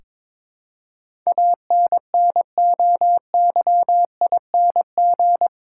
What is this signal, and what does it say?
Morse code, heard as a keyed tone, about 22 wpm, 700 Hz. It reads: ANNOYING
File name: practice / 22wpm / Eight Letter Words - 22wpm